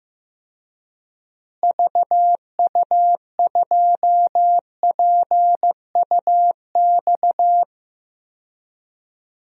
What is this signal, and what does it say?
Morse code, heard as a keyed tone, about 15 wpm, 700 Hz. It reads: VU2PUX
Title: DXCC-100 Call Signs - 6 Characters - Repeated 3x - 15wpm